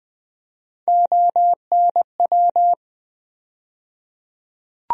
Morse code, heard as a keyed tone, about 20 wpm, 700 Hz. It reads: ONW
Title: USA Call Sign Postfixes - 3 Letters 20wpm